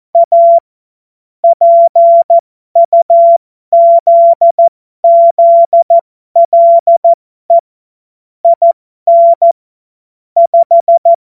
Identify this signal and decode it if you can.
Morse code, heard as a keyed tone, about 14 wpm, spaced as for 12 wpm, 680 Hz. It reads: A PUZZLE IN 5
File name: Morsekurs / Witze